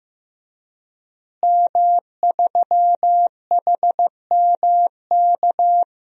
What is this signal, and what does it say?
Morse code, heard as a keyed tone, about 15 wpm, 700 Hz. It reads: M3HMK
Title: DXCC-100 Call Signs - 5 Characters - Repeated 2x - 15wpm